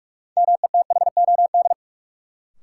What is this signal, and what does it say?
Morse code, heard as a keyed tone, about 43 wpm, 700 Hz. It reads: METHOD
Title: Sentences from Top 600 Words - New Word Drill - 45wpm